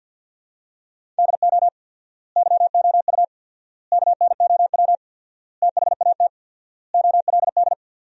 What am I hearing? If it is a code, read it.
Morse code, heard as a keyed tone, about 54 wpm, 700 Hz. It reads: DO YOU KNOW THAT OLD
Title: 4x Word-Spacing - Sentences from Top 400 Words - 50wpm